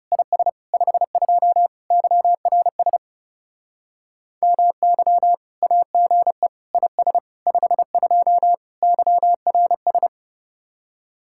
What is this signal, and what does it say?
Morse code, heard as a keyed tone, about 39 wpm, 700 Hz. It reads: IS 52 YRS MY AGE IS 52 YRS